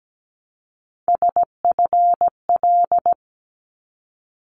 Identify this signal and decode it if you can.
Morse code, heard as a keyed tone, about 17 wpm, 700 Hz. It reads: SFL
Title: ARRL Sections 17wpm